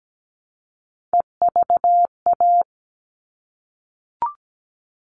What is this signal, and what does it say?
Morse code, heard as a keyed tone, about 17 wpm, 700 Hz. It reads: EVA
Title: All Cut Numbers - 3-Digit 17wpm